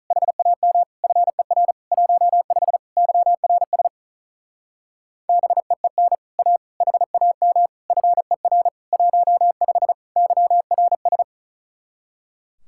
Morse code, heard as a keyed tone, about 44 wpm, 700 Hz. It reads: HAM FER 15 YRS BEEN A HAM FER 15 YRS